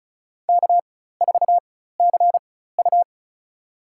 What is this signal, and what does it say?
Morse code, heard as a keyed tone, about 36 wpm, spaced as for 16 wpm, 700 Hz. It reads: K4CU
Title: USA Call Signs - 4x Character Spacing - 35wpm